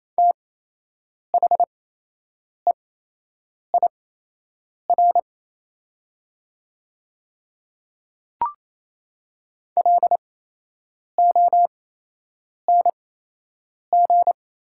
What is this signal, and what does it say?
Morse code, heard as a keyed tone, about 28 wpm, spaced as for 8 wpm, 700 Hz. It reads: THEIR LONG
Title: 8x Character Spacing - Top 100 Words - 28wpm